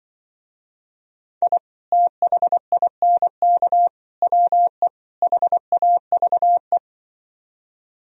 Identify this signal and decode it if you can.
Morse code, heard as a keyed tone, about 24 wpm, 700 Hz. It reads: I THINK WE HAVE
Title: Speed-Racing - Sets of 4 Words 20wpm